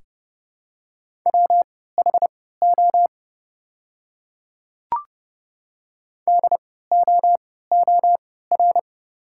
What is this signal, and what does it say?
Morse code, heard as a keyed tone, about 32 wpm, spaced as for 17 wpm, 690 Hz. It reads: WHO DOOR
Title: 3x Character Spacing - Top 300 Words - 30wpm